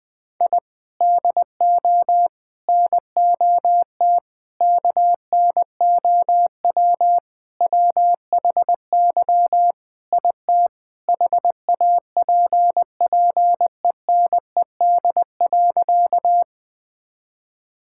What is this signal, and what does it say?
Morse code, heard as a keyed tone, about 20 wpm, 700 Hz. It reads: I DO NOT KNOW WHY IT HAPPENED.
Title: Sentences with 5-Word Phrases - 20wpm